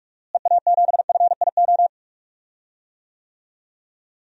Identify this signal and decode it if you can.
Morse code, heard as a keyed tone, about 46 wpm, 700 Hz. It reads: EA7FIO
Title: DXCC-100 Call Signs - 6 Characters - Repeated 2x - 45wpm